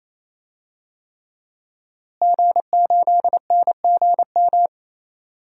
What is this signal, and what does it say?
Morse code, heard as a keyed tone, about 28 wpm, 700 Hz. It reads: G8NGM
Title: DXCC-100 Call Signs - 5 Characters - Repeated 3x - 28wpm